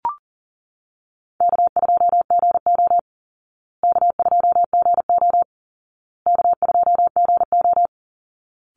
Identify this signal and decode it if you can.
Morse code, heard as a keyed tone, about 37 wpm, 700 Hz. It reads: K2GO K2GO K2GO
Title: Easy Call Signs Repeated 3x - 40wpm